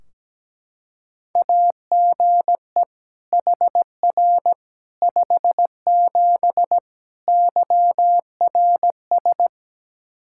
Morse code, heard as a keyed tone, about 17 wpm, 700 Hz. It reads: AGE HR 57 YRS
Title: K1USN SST QSO Practice 17wpm